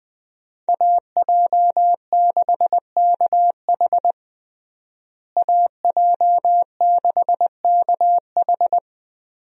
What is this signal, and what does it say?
Morse code, heard as a keyed tone, about 21 wpm, 700 Hz. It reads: AJ6KH AJ6KH
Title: USA Call Signs - 5 Characters - Repeated 3x - 20wpm